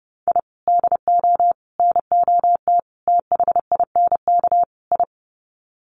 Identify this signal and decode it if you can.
Morse code, heard as a keyed tone, about 30 wpm, 700 Hz. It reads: I DO NOT THINK I
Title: Sets of 5 Words 30wpm